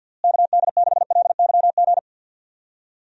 Morse code, heard as a keyed tone, about 50 wpm, 700 Hz. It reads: KD6LYG
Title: USA Call Signs - 6 Characters - Repeated 2x - 50wpm